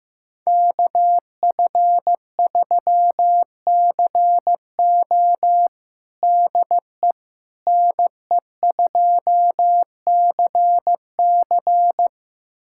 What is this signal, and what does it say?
Morse code, heard as a keyed tone, about 15 wpm, 700 Hz. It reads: KF3CO DE NE2CC
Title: USA Callsign DE Callsign 15wpm